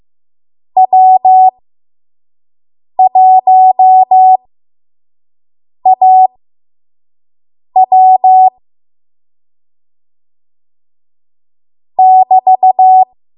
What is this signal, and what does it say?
Morse code, heard as a keyed tone, about 15 wpm, spaced as for 5 wpm, 750 Hz. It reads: W1AW <BT>
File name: Morse / 05WPM